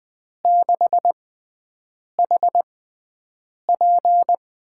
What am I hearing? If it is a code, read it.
Morse code, heard as a keyed tone, about 21 wpm, spaced as for 7 wpm, 700 Hz. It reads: 6HP
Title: USA Call Signs - 5 Characters - 6x Character Spacing - 20wpm